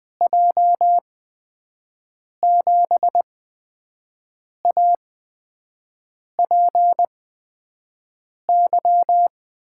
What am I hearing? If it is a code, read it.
Morse code, heard as a keyed tone, about 20 wpm, spaced as for 5 wpm, 700 Hz. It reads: J7APY
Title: USA Call Signs - 6 Characters - 8x Character Spacing - 20wpm